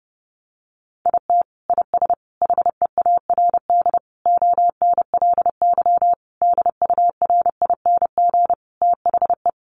Morse code, heard as a keyed tone, about 30 wpm, 700 Hz. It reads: IT IS HEARD ONLY DURING THE
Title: Sentences from Top 1200 Words 30wpm